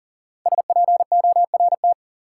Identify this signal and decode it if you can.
Morse code, heard as a keyed tone, about 40 wpm, 700 Hz. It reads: SPORT
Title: Five Letter Words - 40wpm